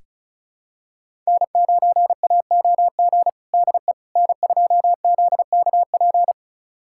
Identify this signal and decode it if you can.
Morse code, heard as a keyed tone, about 36 wpm, 700 Hz. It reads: N9AOG DE N2ZKP
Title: USA Callsign DE Callsign - 5 Characters - 35wpm